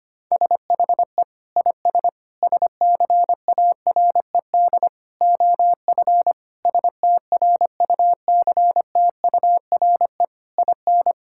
Morse code, heard as a keyed tone, about 25 wpm, 700 Hz. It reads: SHE IS SCARED OF STRUCTURE IN
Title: Sentences from Top 1000 Words - Review Part 6 - 25wpm